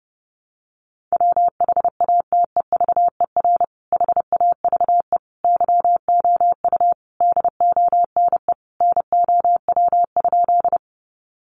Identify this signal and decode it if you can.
Morse code, heard as a keyed tone, about 28 wpm, 700 Hz. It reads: WHATEVER HAVE YOU DONE NOW?
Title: Sentences from Top 1100 Words 30wpm